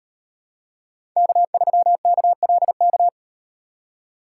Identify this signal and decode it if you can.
Morse code, heard as a keyed tone, about 38 wpm, 700 Hz. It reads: K3KLK